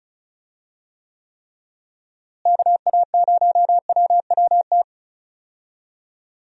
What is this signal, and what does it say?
Morse code, heard as a keyed tone, about 36 wpm, 700 Hz. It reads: KA0WWT
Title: USA Call Signs - 6 Characters - Repeated 3x - 35wpm